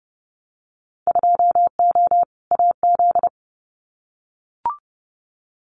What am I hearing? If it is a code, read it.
Morse code, heard as a keyed tone, about 30 wpm, 700 Hz. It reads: 2O AZ
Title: Winter Field Day Exchanges 30wpm